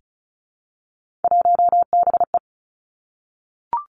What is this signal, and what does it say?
Morse code, heard as a keyed tone, about 36 wpm, 700 Hz. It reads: 1BE
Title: Three Challenging Letter-Number Combinations 35wpm